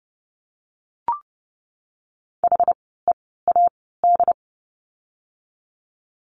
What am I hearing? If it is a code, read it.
Morse code, heard as a keyed tone, about 30 wpm, spaced as for 17 wpm, 700 Hz. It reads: HEAD
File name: Lesson 17f - Letter u - Words 30wpm